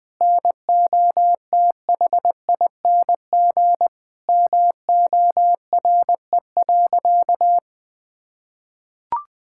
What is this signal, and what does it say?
Morse code, heard as a keyed tone, about 20 wpm, 700 Hz. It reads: NOTHING MORE.
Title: Sentences from Top 1000 Words - Review Part 5 - 20wpm